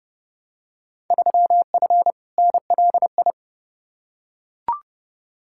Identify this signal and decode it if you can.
Morse code, heard as a keyed tone, about 30 wpm, 700 Hz. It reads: 3F NLI
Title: Field Day Exchanges Encore - 30wpm